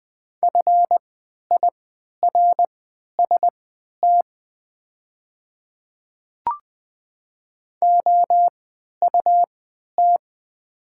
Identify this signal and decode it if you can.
Morse code, heard as a keyed tone, about 20 wpm, spaced as for 11 wpm, 700 Hz. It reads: FIRST OUT
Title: 3x Character Spacing - Top 200 Words - 20wpm